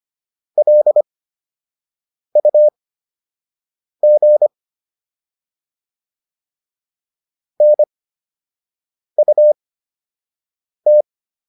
Morse code, heard as a keyed tone, about 25 wpm, spaced as for 6 wpm, 600 Hz. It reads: LUG NUT